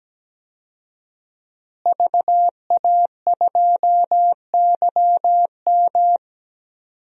Morse code, heard as a keyed tone, about 17 wpm, 700 Hz. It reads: VA2YM
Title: DXCC-100 Call Signs - 5 Characters - Repeated 3x - 17wpm